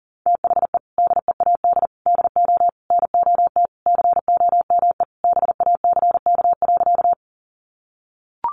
Morse code, heard as a keyed tone, about 40 wpm, 700 Hz. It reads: THE DEAD DO NOT COME BACK.
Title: Sentences from Top 1500 Words 40wpm